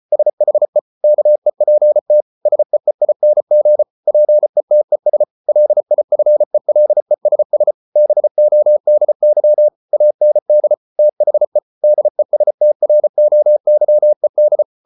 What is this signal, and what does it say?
Morse code, heard as a keyed tone, about 34 wpm, 600 Hz. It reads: SHE KEPT SEEING PETES LIFELESS BODY AND THE DESTROYED